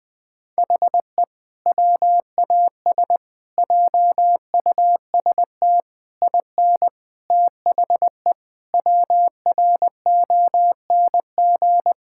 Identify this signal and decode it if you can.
Morse code, heard as a keyed tone, about 20 wpm, 700 Hz. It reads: HE WAS JUST IN THE WRONG
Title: Sentences with 7-Word Phrases - Spoken Beginning - 20wpm